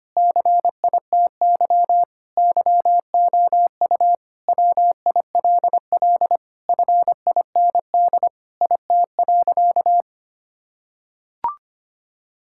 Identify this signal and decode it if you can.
Morse code, heard as a keyed tone, about 25 wpm, 700 Hz. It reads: CITY YOU WILL FIND IT.
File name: practice / 25wpm / Sentences with 3-Word Phrases - 25wpm